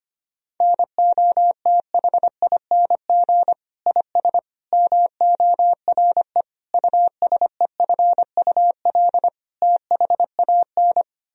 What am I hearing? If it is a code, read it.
Morse code, heard as a keyed tone, about 25 wpm, 700 Hz. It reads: NOTHING IS MORE USEFUL THAN
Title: Sentences from Top 1000 Words - Review Part 4 - 25wpm